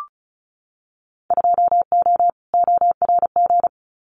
Morse code, heard as a keyed tone, about 36 wpm, 690 Hz. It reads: 2O ORG
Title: Winter Field Day Exchanges 35wpm